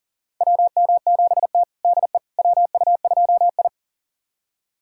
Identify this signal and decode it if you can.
Morse code, heard as a keyed tone, about 37 wpm, 700 Hz. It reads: WM7T DE WU2I
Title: USA Callsign DE Callsign - 4 Characters - 40wpm